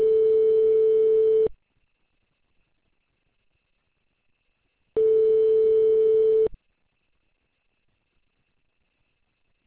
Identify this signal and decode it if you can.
Ringback tone (France), 440 Hz.